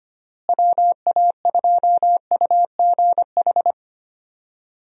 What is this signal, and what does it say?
Morse code, heard as a keyed tone, about 25 wpm, 700 Hz. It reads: WA2UGH
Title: USA Call Signs - 6 Characters - Repeated 2x - 25wpm